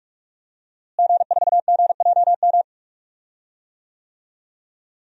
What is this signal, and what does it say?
Morse code, heard as a keyed tone, about 48 wpm, 700 Hz. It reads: G4GJM
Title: DXCC-100 Call Signs - 5 Characters - Repeated 2x - 45wpm